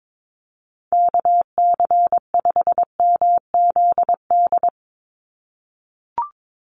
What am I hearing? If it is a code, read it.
Morse code, heard as a keyed tone, about 22 wpm, 700 Hz. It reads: KC5MZD
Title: USA Call Signs 22wpm